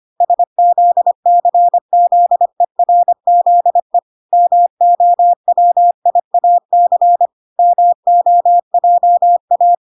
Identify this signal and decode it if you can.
Morse code, heard as a keyed tone, about 25 wpm, 700 Hz. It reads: SZCZERZE MOWIAC MOJA
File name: recordings / 25wpm0000